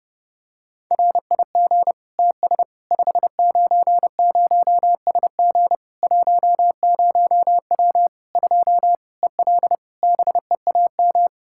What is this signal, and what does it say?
Morse code, heard as a keyed tone, about 30 wpm, 700 Hz. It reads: RIG TS 590SG 10W 2 EL BEAM
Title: Practice QSO - Exchange 2 - Info - 30wpm